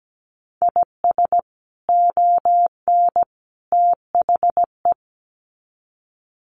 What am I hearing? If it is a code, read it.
Morse code, heard as a keyed tone, about 17 wpm, 700 Hz. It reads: IS ON THE
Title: Sets of 3 Words 17wpm